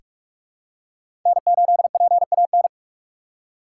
Morse code, heard as a keyed tone, about 46 wpm, 700 Hz. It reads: N8PAN
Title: USA Call Signs - 5 Characters - 2x Character Spacing - 45wpm